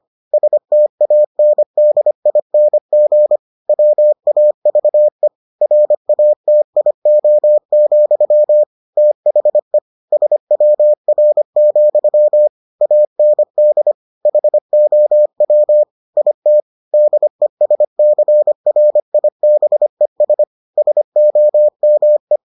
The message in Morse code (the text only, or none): STANDING WAVE RATIO, THE SWR, AND HOW IT DESCRIBES SOME